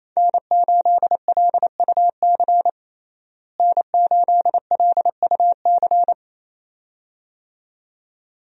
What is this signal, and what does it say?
Morse code, heard as a keyed tone, about 28 wpm, 700 Hz. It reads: N8LUC N8LUC
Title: USA Call Signs - 5 Characters - Repeated 2x - 28wpm